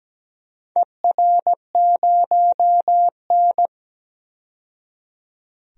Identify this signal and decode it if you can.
Morse code, heard as a keyed tone, about 17 wpm, 700 Hz. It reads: ER0N
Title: DXCC-100 Call Signs - 4 Characters - 17wpm